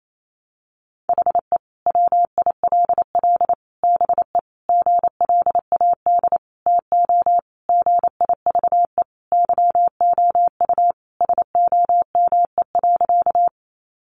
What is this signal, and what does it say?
Morse code, heard as a keyed tone, about 28 wpm, 700 Hz. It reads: HE WILL BE GLAD TO GIVE YOU SOME.
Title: Sentences from Top 800 Words 28wpm